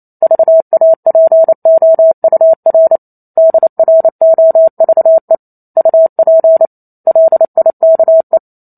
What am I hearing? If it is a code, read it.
Morse code, heard as a keyed tone, about 28 wpm, 650 Hz. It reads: VAPOUR DROVE UP LIKE
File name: Chapter 2 - 28 WPM